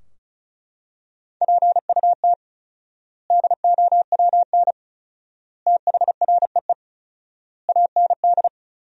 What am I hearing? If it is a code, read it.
Morse code, heard as a keyed tone, about 34 wpm, 700 Hz. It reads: PUT DOWN THREE AND
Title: 4x Word-Spacing - Sentences from Top 400 Words - 35wpm